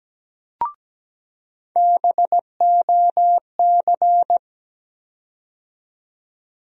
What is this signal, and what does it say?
Morse code, heard as a keyed tone, about 17 wpm, 700 Hz. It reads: BOC
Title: USA Call Sign Postfixes - 3 Letters 17wpm